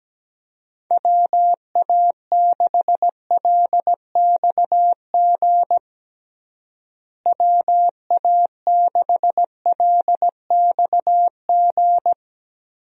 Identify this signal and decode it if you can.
Morse code, heard as a keyed tone, about 17 wpm, 700 Hz. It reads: WA6LXG WA6LXG
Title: USA Call Signs - 6 Characters - Repeated 3x - 17wpm